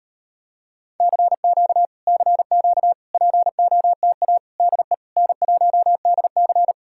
Morse code, heard as a keyed tone, about 34 wpm, 700 Hz. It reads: CQ CQ POTA DE N1DC